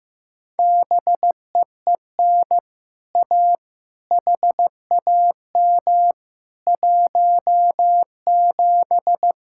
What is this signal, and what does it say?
Morse code, heard as a keyed tone, about 16 wpm, 700 Hz. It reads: BEEN A HAM 17
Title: Speed-Racing Common QSO Phrases Encore - 15wpm